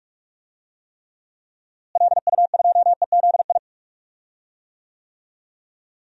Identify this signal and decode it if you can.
Morse code, heard as a keyed tone, about 47 wpm, 700 Hz. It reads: LU2EZI
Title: DXCC-100 Call Signs - 6 Characters - Repeated 2x - 45wpm